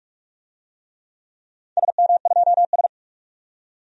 Morse code, heard as a keyed tone, about 46 wpm, 700 Hz. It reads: SM2S
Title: DXCC-100 Call Signs - 4 Characters - 45wpm